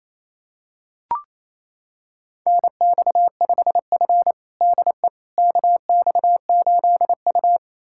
Morse code, heard as a keyed tone, about 28 wpm, 700 Hz. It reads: NX5F DE KX8U
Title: USA Callsign DE Callsign - 4 Characters - 28wpm